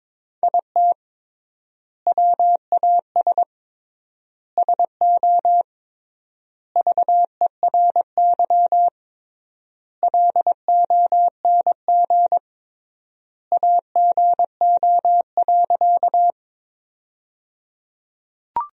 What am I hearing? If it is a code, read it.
Morse code, heard as a keyed tone, about 23 wpm, 700 Hz. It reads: IT WAS SO VERY LONG AGO.
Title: 3x Word-Spacing - Sentences from Top 500 Words - 22wpm